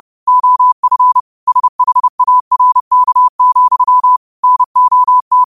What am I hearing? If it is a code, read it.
Morse code, heard as a keyed tone, about 30 wpm, 1000 Hz. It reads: OF SHARK, NOT
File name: qotdfiles / 30wpm